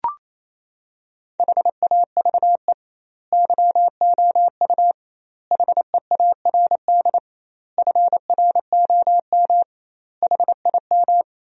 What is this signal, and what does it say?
Morse code, heard as a keyed tone, about 28 wpm, 700 Hz. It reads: HAVE YOU HEARD FROM HIM
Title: 2x Word-Spacing - Sentences from Top 400 Words - 28wpm